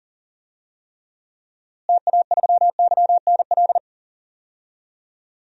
Morse code, heard as a keyed tone, about 40 wpm, 700 Hz. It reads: TA3YNL
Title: DXCC-100 Call Signs - 6 Characters - Repeated 2x - 40wpm